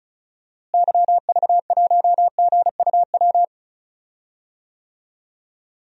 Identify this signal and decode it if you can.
Morse code, heard as a keyed tone, about 36 wpm, 700 Hz. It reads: YV1GUW
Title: DXCC-100 Call Signs - 6 Characters - Repeated 3x - 35wpm